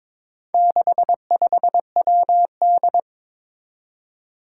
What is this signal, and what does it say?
Morse code, heard as a keyed tone, about 22 wpm, 700 Hz. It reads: 65WD
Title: Four Challenging Letter-Number Combinations 22wpm